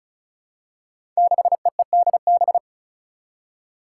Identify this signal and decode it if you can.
Morse code, heard as a keyed tone, about 36 wpm, 700 Hz. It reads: 6EEDB
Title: Five Challenging Letter-Number Combinations 35wpm